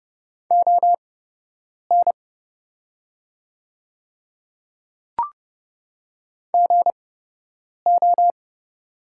Morse code, heard as a keyed tone, about 31 wpm, spaced as for 8 wpm, 700 Hz. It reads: ON GO
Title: 8x Character Spacing - Top 100 Words - 30wpm